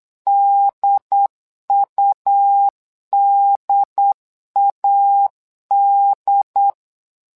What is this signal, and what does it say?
Morse code, heard as a keyed tone, about 8 wpm, 800 Hz. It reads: DUDAD
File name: LEZIONI / 2 UHDN / 2 UHD 8wpm 2258